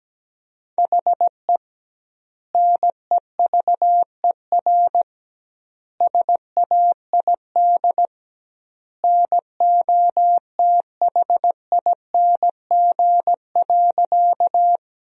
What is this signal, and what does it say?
Morse code, heard as a keyed tone, about 17 wpm, 700 Hz. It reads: HE NEVER SAID NOTHING.
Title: 2x Word-Spacing - Sentences from Top 500 Words - 17wpm